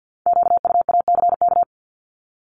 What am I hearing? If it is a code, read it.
Morse code, heard as a keyed tone, about 50 wpm, 700 Hz. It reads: QUACK
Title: Multi-letter Phonemes 50wpm